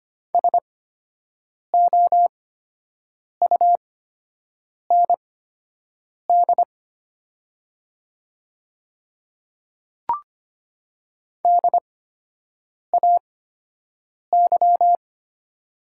Morse code, heard as a keyed tone, about 25 wpm, spaced as for 7 wpm, 700 Hz. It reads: SOUND DAY
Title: 8x Character Spacing - Top 100 Words - 25wpm